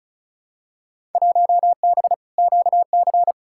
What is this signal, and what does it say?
Morse code, heard as a keyed tone, about 34 wpm, 700 Hz. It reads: 1B QC
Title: Field Day Exchanges Encore - 35wpm